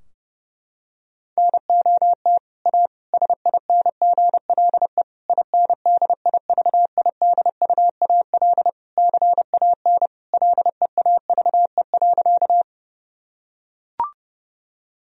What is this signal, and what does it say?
Morse code, heard as a keyed tone, about 30 wpm, 700 Hz. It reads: NOT A SINGLE INDIVIDUAL CAN LEAVE.
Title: Sentences from Top 1000 Words - Review Part 4 - 30wpm